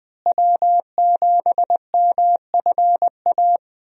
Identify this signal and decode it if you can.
Morse code, heard as a keyed tone, about 20 wpm, 700 Hz. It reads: W7MFA
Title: USA Call Signs - 5 Characters - Repeated 2x - 20wpm